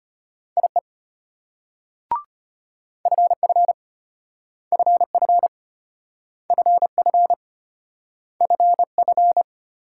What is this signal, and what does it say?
Morse code, heard as a keyed tone, about 32 wpm, 700 Hz. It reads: IE FF FF FF FF